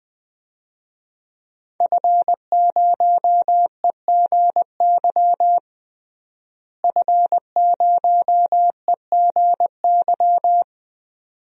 Morse code, heard as a keyed tone, about 20 wpm, 700 Hz. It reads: F0EGY F0EGY
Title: DXCC-100 Call Signs - 5 Characters - Repeated 2x - 20wpm